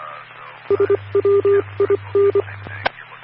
Morse code, heard as a keyed tone, about 24 wpm, 400 Hz. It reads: SWIN
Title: Claremont, CA Police, Amateur Radio